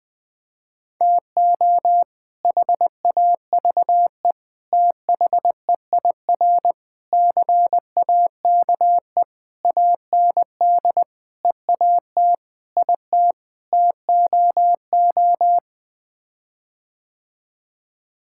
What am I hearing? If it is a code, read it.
Morse code, heard as a keyed tone, about 19 wpm, 700 Hz. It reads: TO HAVE THEIR CAKE AND EAT IT TOO
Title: Sets of 8 Words - 20wpm